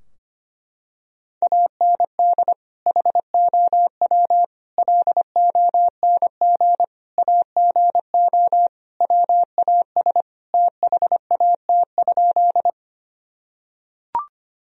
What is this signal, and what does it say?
Morse code, heard as a keyed tone, about 25 wpm, 700 Hz. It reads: AND HOW LONG AGO WAS THAT?
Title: Sentences from Top 1000 Words - Review Part 3 - 25wpm